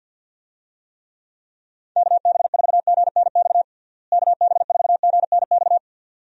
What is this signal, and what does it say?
Morse code, heard as a keyed tone, about 51 wpm, 700 Hz. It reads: KB4GNX KB4GNX
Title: USA Call Signs - 6 Characters - Repeated 2x - 50wpm